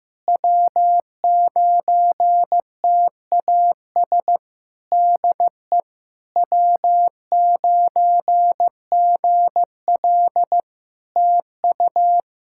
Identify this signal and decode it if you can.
Morse code, heard as a keyed tone, about 14 wpm, 700 Hz. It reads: W9TAS DE W9GL TU
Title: Practice QSO - Exchange 4 - Ending - 15wpm